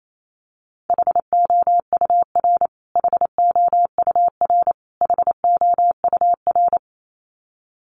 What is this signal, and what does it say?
Morse code, heard as a keyed tone, about 28 wpm, 700 Hz. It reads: HOUR HOUR HOUR
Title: Top 500 Words Repeated 28wpm